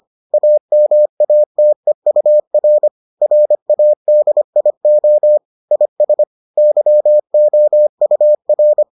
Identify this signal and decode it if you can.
Morse code, heard as a keyed tone, about 25 wpm, 600 Hz. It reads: AMATEUR RADIO IS YOUR